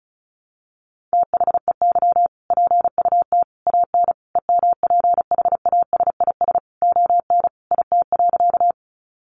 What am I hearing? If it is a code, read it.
Morse code, heard as a keyed tone, about 34 wpm, 700 Hz. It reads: THEY PUT AN EMPHASIS ON IT.
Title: Sentences from Top 1400 Words 35wpm